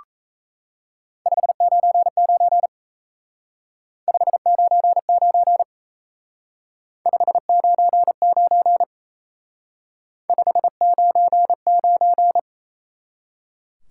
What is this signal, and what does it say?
Morse code, heard as a keyed tone, about 33 wpm, 700 Hz. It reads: 599 599 599 599
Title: Speed-Racing Top 90 QSO Elements 28wpm